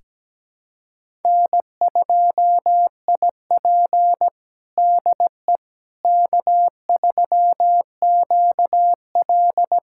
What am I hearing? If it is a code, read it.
Morse code, heard as a keyed tone, about 17 wpm, 700 Hz. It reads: N2IP DE K3QL
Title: USA Callsign DE Callsign - 4 Characters - 17wpm